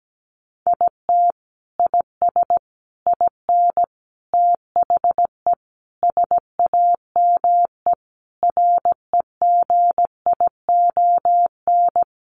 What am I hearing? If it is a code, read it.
Morse code, heard as a keyed tone, about 17 wpm, 700 Hz. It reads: IT IS IN THE SAME REGION
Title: Sentences from Top 1500 Words 17wpm